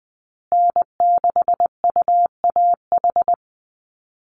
Morse code, heard as a keyed tone, about 20 wpm, 700 Hz. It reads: N6UAH
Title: USA Call Signs Repeated 2x 20wpm